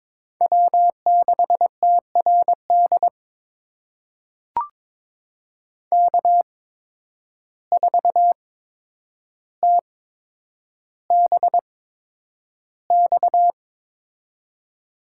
Morse code, heard as a keyed tone, about 22 wpm, spaced as for 9 wpm, 700 Hz. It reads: W6TRD K4TBX
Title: USA Call Signs - 5 Characters - 8x Character Spacing - 22wpm